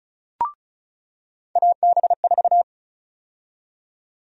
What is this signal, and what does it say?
Morse code, heard as a keyed tone, about 36 wpm, 700 Hz. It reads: AB4
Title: USA Call Sign Prefixes - 35wpm